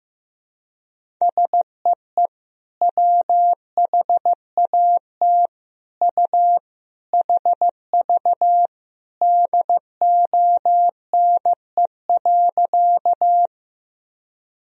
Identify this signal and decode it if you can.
Morse code, heard as a keyed tone, about 15 wpm, 700 Hz. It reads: SEE WHAT U HV DONE.
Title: Sentences from Top 500 Words and Abbreviations 15wpm